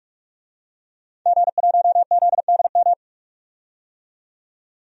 Morse code, heard as a keyed tone, about 43 wpm, 700 Hz. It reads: G1ZDM
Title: DXCC-100 Call Signs - 5 Characters - 45wpm